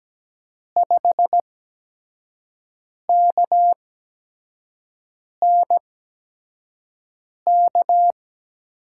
Morse code, heard as a keyed tone, about 17 wpm, spaced as for 5 wpm, 700 Hz. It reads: 5KNK
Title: USA Call Signs - 8x Character Spacing - 17wpm